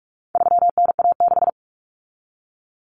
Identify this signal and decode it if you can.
Morse code, heard as a keyed tone, about 44 wpm, 700 Hz. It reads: 3NA6
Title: Common Cut Numbers - 4-Digit 45wpm